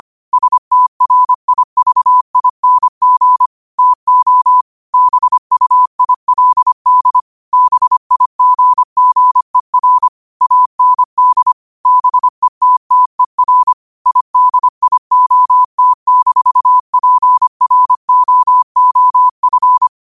Morse code, heard as a keyed tone, about 25 wpm, 1000 Hz. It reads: STRIVING TO BUILD BIGGER AND BETTER IDIOT-PROOF
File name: qotdfiles / 25wpm